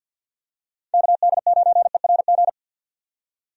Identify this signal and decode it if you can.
Morse code, heard as a keyed tone, about 47 wpm, 700 Hz. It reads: KD9ERG
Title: USA Call Signs - 6 Characters - 50wpm